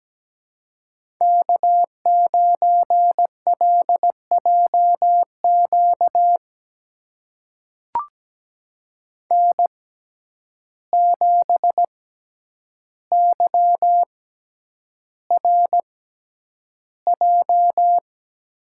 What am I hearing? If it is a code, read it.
Morse code, heard as a keyed tone, about 17 wpm, spaced as for 9 wpm, 700 Hz. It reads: K9LJQ N7YRJ
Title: USA Call Signs - 5 Characters - 6x Character Spacing - 17wpm